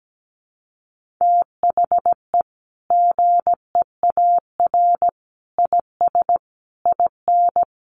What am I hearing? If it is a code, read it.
Morse code, heard as a keyed tone, about 17 wpm, 700 Hz. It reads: THE GEAR IS IN
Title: Sentences from Top 1400 Words 17wpm